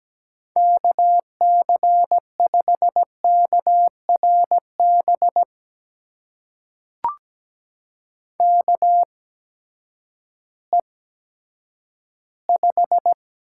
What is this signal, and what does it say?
Morse code, heard as a keyed tone, about 17 wpm, spaced as for 10 wpm, 700 Hz. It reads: KC5KRB KE5
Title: USA Call Signs - 6 Characters - 8x Character Spacing - 17wpm